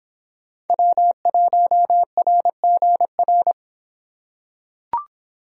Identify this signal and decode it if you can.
Morse code, heard as a keyed tone, about 24 wpm, 700 Hz. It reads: W1RGR